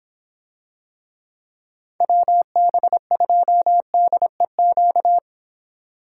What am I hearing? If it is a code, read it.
Morse code, heard as a keyed tone, about 25 wpm, 700 Hz. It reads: WB2DEQ